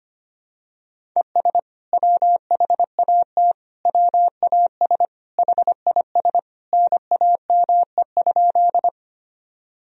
Morse code, heard as a keyed tone, about 25 wpm, 700 Hz. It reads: ES WHAT WAS HIS NAME?
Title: Sentences from Top 200 Words and Abbreviations 25wpm